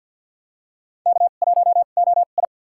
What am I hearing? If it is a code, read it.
Morse code, heard as a keyed tone, about 52 wpm, spaced as for 36 wpm, 700 Hz. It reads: K1OI
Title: USA Call Signs - 2x Character Spacing - 50wpm